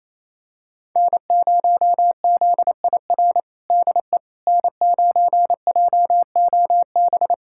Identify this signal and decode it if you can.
Morse code, heard as a keyed tone, about 27 wpm, 700 Hz. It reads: N0ZIR DE N9JOB
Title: USA Callsign DE Callsign - 5 Characters - 28wpm